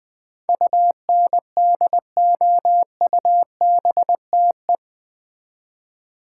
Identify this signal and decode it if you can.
Morse code, heard as a keyed tone, about 20 wpm, 700 Hz. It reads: UNDOUBTE
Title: Build A Word - 4 Letters At A Time - 11-Letter Words - 20wpm